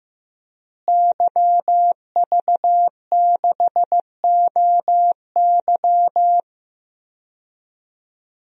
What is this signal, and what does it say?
Morse code, heard as a keyed tone, about 15 wpm, 700 Hz. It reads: YV6OY
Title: DXCC-100 Call Signs - 5 Characters - Repeated 2x - 15wpm